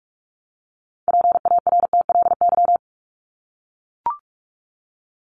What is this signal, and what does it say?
Morse code, heard as a keyed tone, about 44 wpm, 700 Hz. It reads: PARTLY
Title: LY Suffix - Top 100 Words with Base Words 45wpm